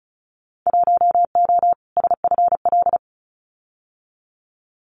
Morse code, heard as a keyed tone, about 34 wpm, 700 Hz. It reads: 1O SFL
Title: Winter Field Day Exchanges Encore 35wpm